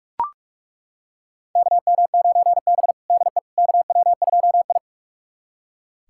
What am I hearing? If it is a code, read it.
Morse code, heard as a keyed tone, about 41 wpm, 700 Hz. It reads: KM9B DE KW2I
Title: USA Callsign DE Callsign - 4 Characters - 45wpm